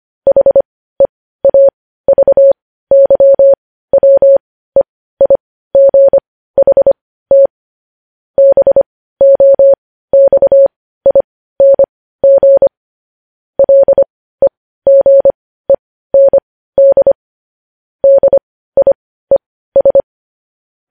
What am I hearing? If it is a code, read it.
Morse code, heard as a keyed tone, about 25 wpm, spaced as for 15 wpm, 550 Hz. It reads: HEAVYWEIGHT BOXING LEGEND DIES
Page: news headlines in morse code at 15 wpm